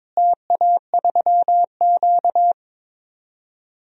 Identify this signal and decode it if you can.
Morse code, heard as a keyed tone, about 21 wpm, 700 Hz. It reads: TA3Q
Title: DXCC-100 Call Signs - 4 Characters - Repeated 3x - 22wpm